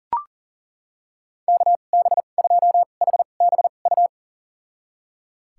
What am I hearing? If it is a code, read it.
Morse code, heard as a keyed tone, about 40 wpm, spaced as for 29 wpm, 700 Hz. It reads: KB2HBU
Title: USA Call Signs - 6 Characters - 2x Character Spacing - 40wpm